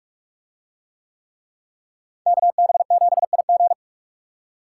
Morse code, heard as a keyed tone, about 42 wpm, 700 Hz. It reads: KB7IG